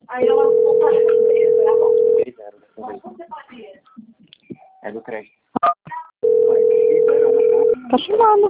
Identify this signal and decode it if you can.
Ringback tone (North America), 440 + 480 Hz, with speech.
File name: out- - - - - .23846